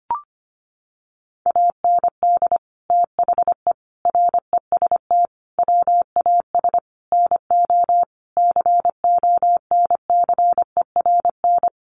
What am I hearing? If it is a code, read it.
Morse code, heard as a keyed tone, about 25 wpm, 700 Hz. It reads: AND THE REST WAS NO CONCERN
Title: Sentences from Top 1000 Words 25wpm